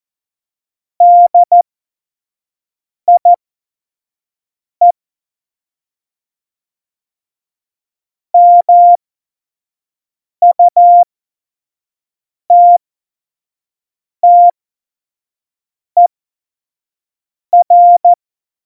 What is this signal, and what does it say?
Morse code, heard as a keyed tone, about 14 wpm, spaced as for 5 wpm, 700 Hz. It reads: DIE MUTTER
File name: Morsekurs / Witze / Witz 19. September 17, Tempo 25